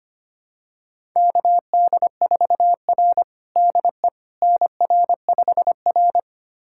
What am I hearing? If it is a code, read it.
Morse code, heard as a keyed tone, about 25 wpm, 700 Hz. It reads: KD4R DE NR5R
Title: USA Callsign DE Callsign - 4 Characters - 25wpm